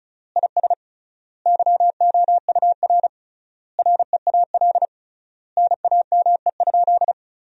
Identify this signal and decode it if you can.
Morse code, heard as a keyed tone, about 34 wpm, 700 Hz. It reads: IS YOUR REAL NAME?
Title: 3x Word-Spacing - Sentences from Top 300 Words - 35wpm